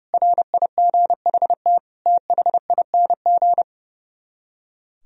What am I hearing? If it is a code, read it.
Morse code, heard as a keyed tone, about 35 wpm, 700 Hz. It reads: RIGHT THING
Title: Speed-Racing - Sets of 5 Words 30wpm